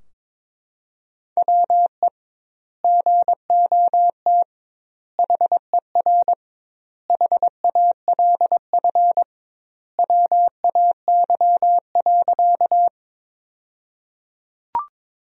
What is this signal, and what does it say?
Morse code, heard as a keyed tone, about 22 wpm, 700 Hz. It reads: WE GOT HER HALF WAY.
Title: 2x Word-Spacing - Sentences from Top 400 Words - 22wpm